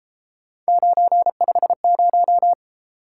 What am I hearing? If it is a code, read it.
Morse code, heard as a keyed tone, about 32 wpm, 700 Hz. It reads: 950